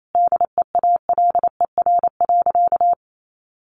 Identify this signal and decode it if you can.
Morse code, heard as a keyed tone, about 28 wpm, 700 Hz. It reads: DEALER.
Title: Sentences from Top 1100 Words 28wpm